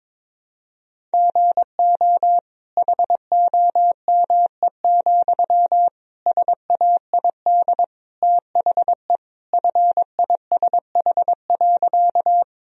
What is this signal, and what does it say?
Morse code, heard as a keyed tone, about 22 wpm, 700 Hz. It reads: GO HOME, SAID THE FISH.
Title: Sentences from Top 1000 Words - Review Part 1 - 22wpm